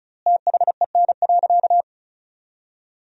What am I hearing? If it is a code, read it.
Morse code, heard as a keyed tone, about 34 wpm, 700 Hz. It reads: THEN.
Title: 3x Word-Spacing - Sentences from Top 300 Words - 35wpm